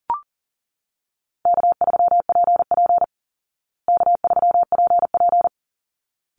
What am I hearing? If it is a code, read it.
Morse code, heard as a keyed tone, about 40 wpm, 700 Hz. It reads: K3PP K3PP
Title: Easy Call Signs Repeated 2x - 40wpm